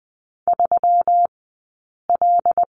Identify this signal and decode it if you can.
Morse code, heard as a keyed tone, about 20 wpm, 700 Hz. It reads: 3 L
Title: 120 Lesson 20d - Number 3 - Two Characters 20wpm